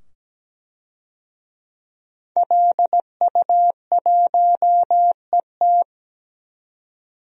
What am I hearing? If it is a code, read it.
Morse code, heard as a keyed tone, about 17 wpm, 700 Hz. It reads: LU1ET
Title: DXCC-100 Call Signs - 5 Characters - 17wpm